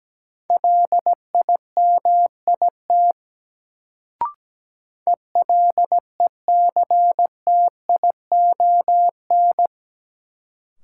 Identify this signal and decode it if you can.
Morse code, heard as a keyed tone, about 17 wpm, 700 Hz. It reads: LIMIT ELECTION
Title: Sentences from Top 1500 Words - New Word Drill - 17wpm